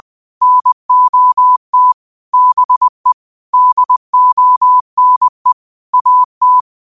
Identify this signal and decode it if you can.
Morse code, heard as a keyed tone, about 20 wpm, 1000 Hz. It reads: NOT BE DONE AT